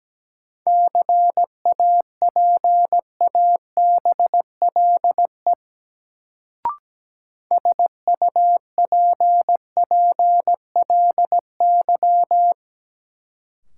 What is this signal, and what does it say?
Morse code, heard as a keyed tone, about 17 wpm, 700 Hz. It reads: CAPABLE SUPPLY
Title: Sentences from Top 1200 Words - New Word Drill Alt - 17wpm